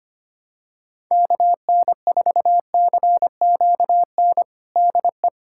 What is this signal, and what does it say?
Morse code, heard as a keyed tone, about 25 wpm, 700 Hz. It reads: KN4CQN DE
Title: USA Callsign DE Callsign - 6 Characters - 25wpm